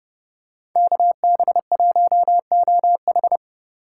Morse code, heard as a keyed tone, about 29 wpm, 700 Hz. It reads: KB1OH